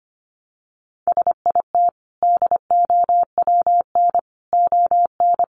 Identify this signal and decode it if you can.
Morse code, heard as a keyed tone, about 25 wpm, 700 Hz. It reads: SIT DOWN ON
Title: Sentences from Top 1400 Words 25wpm